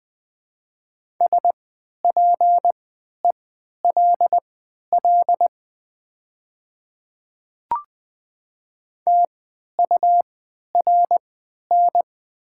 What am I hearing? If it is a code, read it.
Morse code, heard as a keyed tone, about 20 wpm, spaced as for 11 wpm, 700 Hz. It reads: SPELL TURN
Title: 3x Character Spacing - Top 300 Words - 20wpm